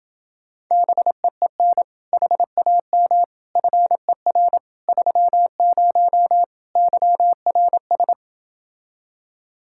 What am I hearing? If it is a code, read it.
Morse code, heard as a keyed tone, about 30 wpm, 700 Hz. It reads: BEEN HAM FER 30 YRS